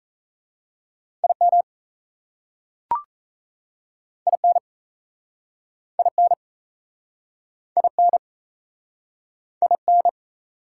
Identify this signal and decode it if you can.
Morse code, heard as a keyed tone, about 37 wpm, 700 Hz. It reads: IM IN IN IN IN